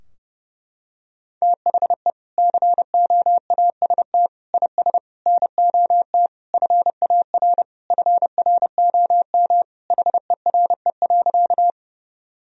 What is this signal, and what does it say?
Morse code, heard as a keyed tone, about 32 wpm, 700 Hz. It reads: THE COAST IS NOT FAR FROM HERE.
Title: Sentences from Top 1000 Words - Review Part 4 - 30wpm